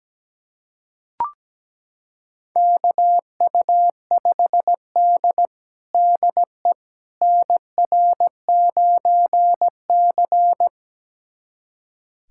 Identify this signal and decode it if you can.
Morse code, heard as a keyed tone, about 17 wpm, 700 Hz. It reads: KU5D DE NR9C
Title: USA Callsign DE Callsign - 4 Characters - 17wpm